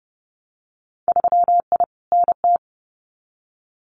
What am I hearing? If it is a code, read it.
Morse code, heard as a keyed tone, about 30 wpm, 700 Hz. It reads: 3I NT
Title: Winter Field Day Exchanges Encore 30wpm